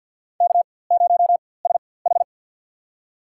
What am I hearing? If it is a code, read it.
Morse code, heard as a keyed tone, about 53 wpm, spaced as for 24 wpm, 700 Hz. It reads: K0SH
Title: USA Call Signs - 4x Character Spacing - 50wpm